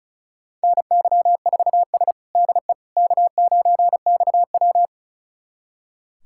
Morse code, heard as a keyed tone, about 36 wpm, 700 Hz. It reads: NY4S DE K9XW
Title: USA Callsign DE Callsign - 4 Characters - 35wpm